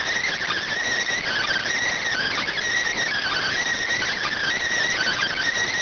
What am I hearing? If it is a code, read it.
Emergency Alert System (SAME) tones (header bursts), heard as harsh buzzing bursts. It is heard from 1.1 to 5.4 s.